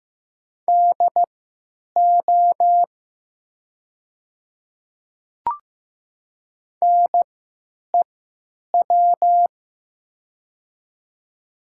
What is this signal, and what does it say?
Morse code, heard as a keyed tone, about 15 wpm, spaced as for 9 wpm, 700 Hz. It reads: DO NEW
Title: 3x Character Spacing - Top 300 Words - 15wpm